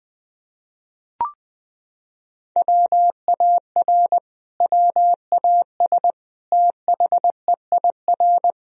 Morse code, heard as a keyed tone, about 20 wpm, 700 Hz. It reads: WAR WAS THEIR
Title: Sentences from Top 1000 Words - Review Part 3 - 20wpm